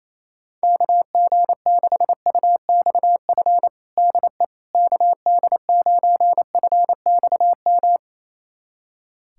Morse code, heard as a keyed tone, about 28 wpm, 700 Hz. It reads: KG6UXF DE KD9FXM
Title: USA Callsign DE Callsign 28wpm